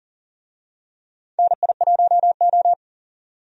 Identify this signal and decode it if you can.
Morse code, heard as a keyed tone, about 37 wpm, 700 Hz. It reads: NI1O
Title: USA Call Signs - 4 Characters - 40wpm